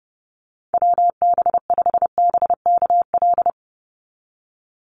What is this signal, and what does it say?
Morse code, heard as a keyed tone, about 32 wpm, 700 Hz. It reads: WB5BKL
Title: NA SOTA Chaser Callsigns 30wpm